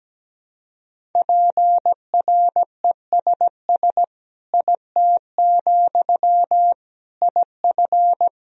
Morse code, heard as a keyed tone, about 17 wpm, 700 Hz. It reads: PRESS IT, IF
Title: Sentences from Top 1000 Words - Review Part 1 - 17wpm